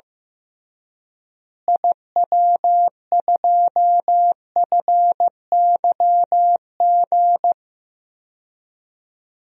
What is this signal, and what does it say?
Morse code, heard as a keyed tone, about 16 wpm, 700 Hz. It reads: IW2FYG
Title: DXCC-100 Call Signs - 6 Characters - Repeated 3x - 15wpm